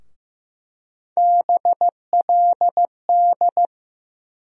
Morse code, heard as a keyed tone, about 15 wpm, 700 Hz. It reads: BLD
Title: USA Call Sign Postfixes - 3 Letters 15wpm